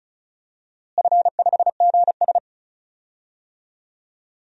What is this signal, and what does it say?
Morse code, heard as a keyed tone, about 34 wpm, 700 Hz. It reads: F5GS
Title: DXCC-100 Call Signs - 4 Characters - Repeated 2x - 35wpm